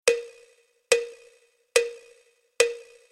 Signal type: none